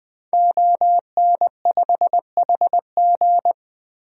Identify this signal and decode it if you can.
Morse code, heard as a keyed tone, about 20 wpm, 700 Hz. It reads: ON5HG
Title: DXCC-100 Call Signs - 5 Characters - Repeated 3x - 20wpm